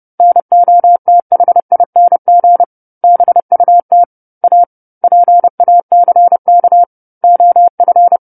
Morse code, heard as a keyed tone, about 30 wpm, 700 Hz. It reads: NOTHING BUT A PACK OF